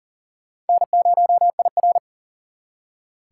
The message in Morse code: N0IR

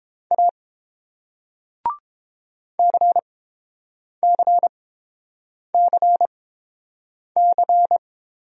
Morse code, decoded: A CCCC

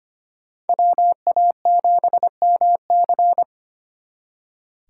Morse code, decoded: WA7MC

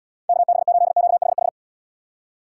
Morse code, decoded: KB8QHH

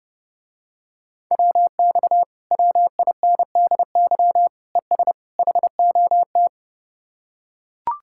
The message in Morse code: WX WINDY ES HOT

